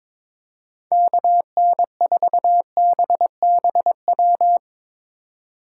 Morse code, heard as a keyed tone, about 22 wpm, 700 Hz. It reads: KN4BBW
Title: Speed Racing - USA 2x3 Call Signs 22wpm